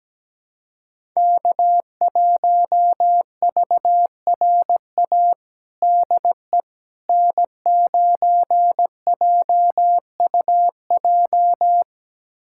Morse code, heard as a keyed tone, about 17 wpm, 700 Hz. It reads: K1VRA DE N9JUJ
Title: USA Callsign DE Callsign - 5 Characters - 17wpm